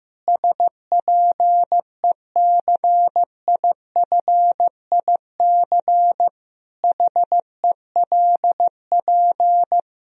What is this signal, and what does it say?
Morse code, heard as a keyed tone, about 14 wpm, 700 Hz. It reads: SPECIFIC HELP
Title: Sentences from Top 1000 Words - Review Part 3 - 15wpm